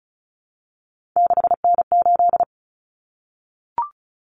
Morse code, heard as a keyed tone, about 34 wpm, 700 Hz. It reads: 6N8
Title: Common Cut Numbers - 3-Digit 35wpm